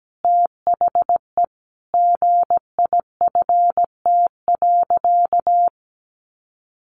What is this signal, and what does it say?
Morse code, heard as a keyed tone, about 17 wpm, 700 Hz. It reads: THE GIFT.
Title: Easy Phrases 17wpm